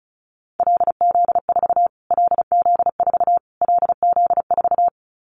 Morse code, heard as a keyed tone, about 34 wpm, 700 Hz. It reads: LZ4 LZ4 LZ4
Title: Callsign Prefixes for Top 25 DXCC - Repeated 3x - 35wpm